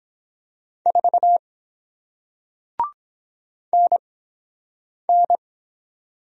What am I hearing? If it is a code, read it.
Morse code, heard as a keyed tone, about 23 wpm, spaced as for 7 wpm, 700 Hz. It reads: 4 NN